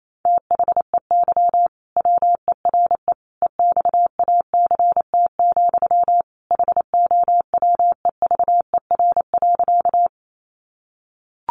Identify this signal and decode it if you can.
Morse code, heard as a keyed tone, about 28 wpm, 700 Hz. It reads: THEY WERE EXACT, HOWEVER.
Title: Sentences from Top 1300 Words 28wpm